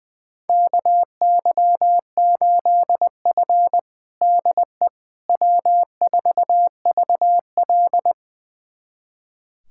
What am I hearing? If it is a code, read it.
Morse code, heard as a keyed tone, about 20 wpm, 700 Hz. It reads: KY8F DE W4VL
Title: USA Callsign DE Callsign - 4 Characters - 20wpm